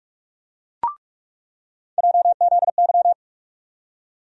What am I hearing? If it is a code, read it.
Morse code, heard as a keyed tone, about 46 wpm, 700 Hz. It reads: JZY
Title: USA Call Sign Postfixes - 3 Letters 45wpm